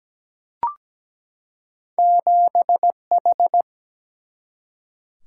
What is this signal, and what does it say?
Morse code, heard as a keyed tone, about 17 wpm, 700 Hz. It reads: 7H
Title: Single-Letter Single-Number 17wpm